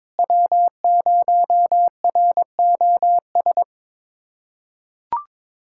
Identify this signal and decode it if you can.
Morse code, heard as a keyed tone, about 21 wpm, 700 Hz. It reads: W0ROS
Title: USA Callsign DE Callsign - 5 Characters - 22wpm